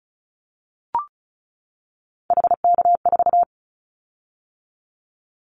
Morse code, heard as a keyed tone, about 36 wpm, 700 Hz. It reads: HK4
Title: Callsign Prefixes for Top 100 DXCC - 35wpm